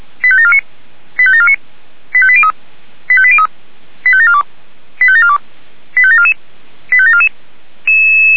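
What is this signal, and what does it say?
Five-tone selective call (ZVEI), heard as quick runs of five tones, tones of about 70 ms. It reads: 85749 85749 85803 85803 85742 85742 85744 85744
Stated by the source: ZVEI five-tone selective call received on an SDR.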